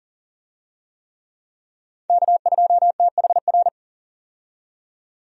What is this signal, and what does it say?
Morse code, heard as a keyed tone, about 40 wpm, 700 Hz. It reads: K2THR
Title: USA Call Signs - 5 Characters - 40wpm